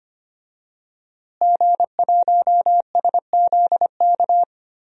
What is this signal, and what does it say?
Morse code, heard as a keyed tone, about 25 wpm, 700 Hz. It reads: G1SZK
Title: DXCC-100 Call Signs - 5 Characters - Repeated 2x - 25wpm